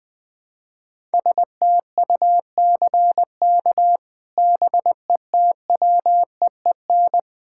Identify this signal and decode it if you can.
Morse code, heard as a keyed tone, about 20 wpm, 700 Hz. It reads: STUCK BETWEEN
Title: Sets of 8 Words - 20wpm